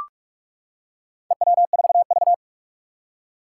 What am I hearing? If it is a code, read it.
Morse code, heard as a keyed tone, about 42 wpm, 700 Hz. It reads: EW4V
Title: Four Challenging Letter-Number Combinations 45wpm